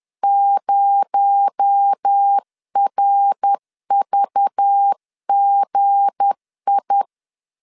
Morse code, heard as a keyed tone, about 10 wpm, 800 Hz. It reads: 0RVGI
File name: LEZIONI / 7 GF9Z / 7 G 10wpm 2501